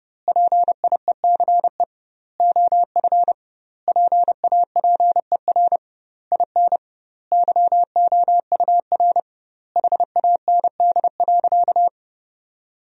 Morse code, heard as a keyed tone, about 32 wpm, 700 Hz. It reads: PIECE OF PAPER IN YOUR HAND.
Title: 2x Word-Spacing - Sentences from Top 400 Words - 30wpm